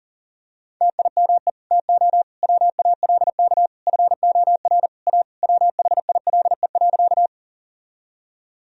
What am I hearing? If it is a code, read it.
Morse code, heard as a keyed tone, about 40 wpm, 700 Hz. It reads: TIME TO WALK FOR A WHILE.
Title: Sentences from Top 1000 Words - Review Part 1 - 40wpm